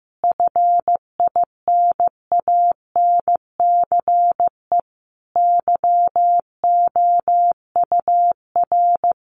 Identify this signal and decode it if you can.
Morse code, heard as a keyed tone, about 15 wpm, 700 Hz. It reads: FINANCE YOUR
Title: Sentences from Top 1500 Words 15wpm